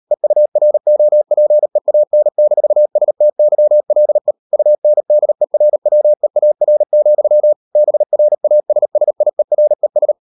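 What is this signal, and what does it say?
Morse code, heard as a keyed tone, about 38 wpm, 600 Hz. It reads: EUROPEAN-STYLE UNDERWEAR, BRASSIERES
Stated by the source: Morse code practice at 38 WPM, sending Ken Follett's novel Never in English, from CW med Gnister.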